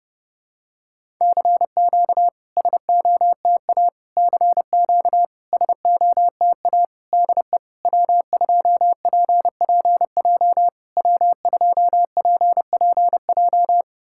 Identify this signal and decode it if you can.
Morse code, heard as a keyed tone, about 30 wpm, 690 Hz. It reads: CQ SOTA CQ SOTA DE W2PPJ W2PPJ
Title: SOTA QSO Practice 30wpm